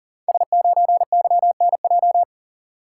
Morse code, heard as a keyed tone, about 41 wpm, 690 Hz. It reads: S9YNJ
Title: Code Groups - 5 Letter:Numbers - 40wpm